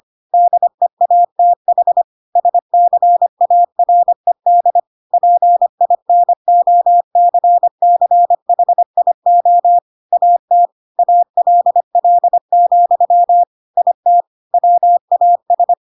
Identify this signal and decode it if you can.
Morse code, heard as a keyed tone, about 25 wpm, 700 Hz. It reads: DEATH SCARED PINOCCHIO AT ALL, IT WAS
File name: Pinocchio / Pinocchio ch5 25 25 ik1hge